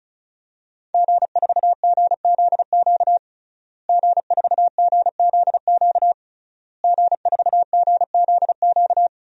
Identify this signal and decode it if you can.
Morse code, heard as a keyed tone, about 36 wpm, 700 Hz. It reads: G4GZQ G4GZQ G4GZQ
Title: DXCC-100 Call Signs - 5 Characters - Repeated 3x - 35wpm